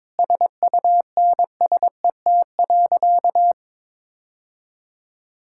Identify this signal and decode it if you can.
Morse code, heard as a keyed tone, about 22 wpm, 700 Hz. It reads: SUNSET.
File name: Sentences with 7-Word Phrases - 22wpm